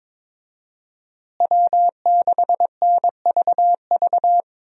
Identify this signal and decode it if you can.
Morse code, heard as a keyed tone, about 22 wpm, 700 Hz. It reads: W6NVV
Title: USA Call Signs - 5 Characters - Repeated 3x - 22wpm